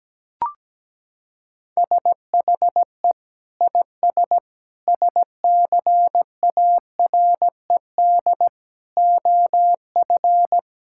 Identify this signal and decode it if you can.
Morse code, heard as a keyed tone, about 17 wpm, 700 Hz. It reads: SHE IS SCARED OF
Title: Sentences from Top 1000 Words - Review Part 6 - 17wpm